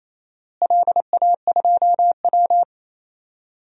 Morse code, heard as a keyed tone, about 28 wpm, 700 Hz. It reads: LA2W
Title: DXCC-100 Call Signs - 4 Characters - Repeated 3x - 28wpm